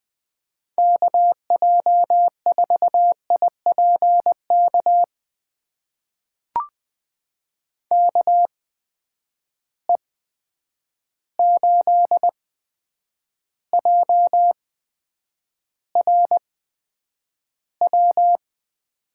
Morse code, heard as a keyed tone, about 20 wpm, spaced as for 9 wpm, 700 Hz. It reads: KJ4IPK KE8JRW
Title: USA Call Signs - 8x Character Spacing - 20wpm